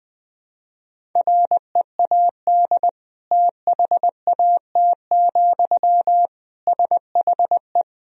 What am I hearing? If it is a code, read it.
Morse code, heard as a keyed tone, about 20 wpm, 700 Hz. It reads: READ THAT, SHE
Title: Sentences from Top 200 Words and Abbreviations 20wpm